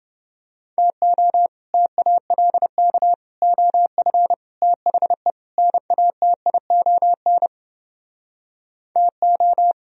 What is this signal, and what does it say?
Morse code, heard as a keyed tone, about 28 wpm, 700 Hz. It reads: TO TALK OF THE NATION TO